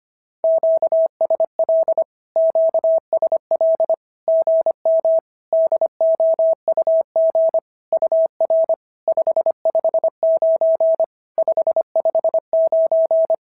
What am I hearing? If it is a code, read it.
Morse code, heard as a keyed tone, about 25 wpm, 650 Hz. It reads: QSL QSL GM DOUG UR 559 559
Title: SOTA QSO Practice 25wpm